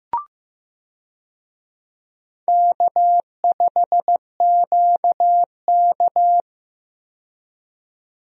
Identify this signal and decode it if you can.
Morse code, heard as a keyed tone, about 15 wpm, 700 Hz. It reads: K5QK
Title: USA Call Signs - 4 Characters - 15wpm